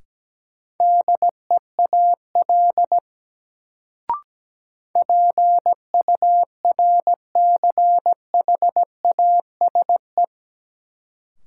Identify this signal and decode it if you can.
Morse code, heard as a keyed tone, about 17 wpm, 700 Hz. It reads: DEAL PURCHASE
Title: Sentences from Top 800 Words - New Word Drill - 17wpm